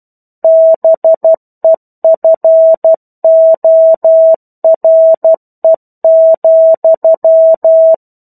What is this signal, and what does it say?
Morse code, heard as a keyed tone, about 12 wpm, 650 Hz. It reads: BEFORE,
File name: Chapter-01 12 WPM